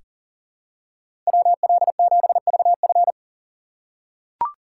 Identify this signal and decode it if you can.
Morse code, heard as a keyed tone, about 37 wpm, 700 Hz. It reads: WL7VF
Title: USA Call Signs - 5 Characters - 3x Character Spacing - 40wpm